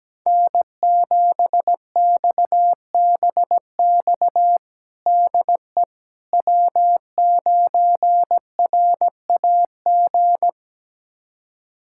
Morse code, heard as a keyed tone, about 17 wpm, 700 Hz. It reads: N7XBX DE W9RAG
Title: USA Callsign DE Callsign - 5 Characters - 17wpm